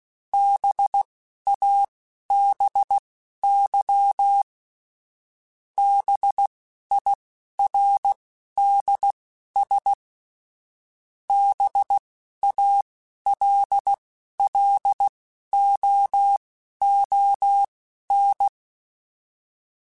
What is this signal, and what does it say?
Morse code, heard as a keyed tone, about 16 wpm, spaced as for 12 wpm, 800 Hz. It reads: BABY BIRDS BALLOON